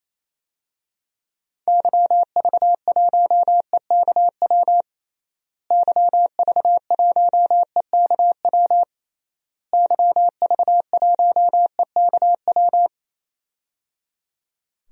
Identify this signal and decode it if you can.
Morse code, heard as a keyed tone, about 28 wpm, 700 Hz. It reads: YV1EKW YV1EKW YV1EKW
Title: DXCC-100 Call Signs - 6 Characters - Repeated 3x - 28wpm